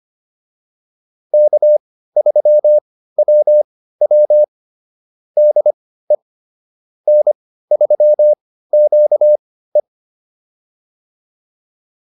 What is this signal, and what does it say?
Morse code, heard as a keyed tone, about 25 wpm, spaced as for 15 wpm, 600 Hz. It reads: K3WW DE N3QE